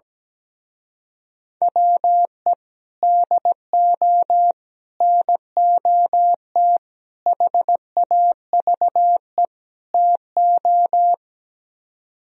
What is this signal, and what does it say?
Morse code, heard as a keyed tone, about 17 wpm, 700 Hz. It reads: WE DO NOT HAVE TO
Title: Speed-Racing - Sets of 5 Words 17wpm